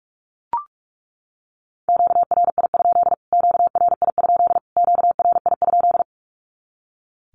Morse code, heard as a keyed tone, about 44 wpm, 700 Hz. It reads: QRI? QRI? QRI?
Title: Q-Codes Repeated 3x 45wpm